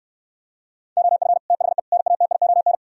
Morse code, heard as a keyed tone, about 69 wpm, 700 Hz. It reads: OF THE NATION